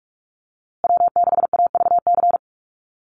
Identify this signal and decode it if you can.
Morse code, heard as a keyed tone, about 44 wpm, 700 Hz. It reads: W6AVC
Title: USA Call Signs Repeated 3x 45wpm